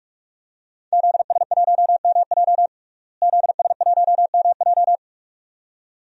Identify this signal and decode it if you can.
Morse code, heard as a keyed tone, about 48 wpm, 700 Hz. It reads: ZS1MJ ZS1MJ
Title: DXCC-100 Call Signs - 5 Characters - Repeated 2x - 45wpm